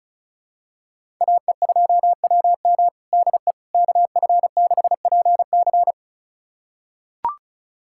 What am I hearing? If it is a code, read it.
Morse code, heard as a keyed tone, about 36 wpm, 700 Hz. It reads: AE2WM DE KF6PC
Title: USA Callsign DE Callsign - 5 Characters - 35wpm